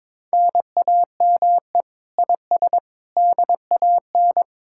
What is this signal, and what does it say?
Morse code, heard as a keyed tone, about 26 wpm, 700 Hz. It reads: NAME IS DAN
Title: Speed-Racing Common QSO Phrases Encore - 22wpm